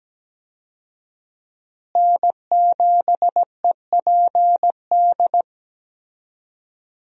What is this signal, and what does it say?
Morse code, heard as a keyed tone, about 17 wpm, 700 Hz. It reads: N7EPD
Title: USA Call Signs - 5 Characters - Repeated 2x - 17wpm